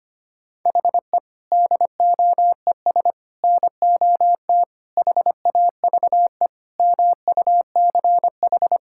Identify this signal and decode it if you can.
Morse code, heard as a keyed tone, about 25 wpm, 700 Hz. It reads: HE DOES NOT HAVE MUCH
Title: Sentences from Top 1000 Words - Review Part 5 - 25wpm